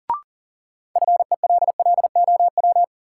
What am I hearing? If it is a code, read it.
Morse code, heard as a keyed tone, about 40 wpm, 700 Hz. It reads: FELLOW
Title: Sets of 5 Words Encore - New Word Drill - 40wpm